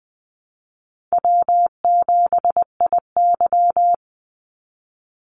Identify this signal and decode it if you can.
Morse code, heard as a keyed tone, about 20 wpm, 700 Hz. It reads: W7IY
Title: Easy Call Signs Repeated 3x - 20wpm